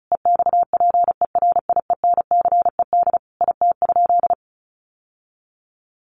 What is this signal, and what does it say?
Morse code, heard as a keyed tone, about 34 wpm, 700 Hz. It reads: EXPERIENCED IT?
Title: Sentences from Top 1100 Words 35wpm